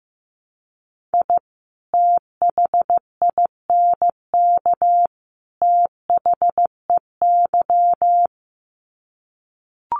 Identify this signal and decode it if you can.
Morse code, heard as a keyed tone, about 15 wpm, 700 Hz. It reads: I THINK THEY
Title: Sets of 3 Words 15wpm